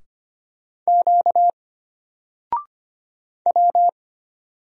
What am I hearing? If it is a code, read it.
Morse code, heard as a keyed tone, about 25 wpm, 700 Hz. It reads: Q W
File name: Single-Letter-Number-25wpm